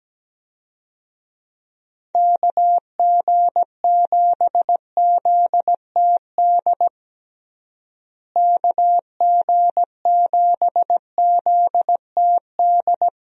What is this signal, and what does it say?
Morse code, heard as a keyed tone, about 17 wpm, 700 Hz. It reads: KG7ZTD KG7ZTD
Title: USA Call Signs - 6 Characters - Repeated 3x - 17wpm